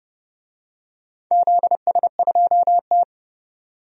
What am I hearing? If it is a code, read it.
Morse code, heard as a keyed tone, about 29 wpm, 700 Hz. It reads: ZS2T
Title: DXCC-100 Call Signs - 4 Characters - 30wpm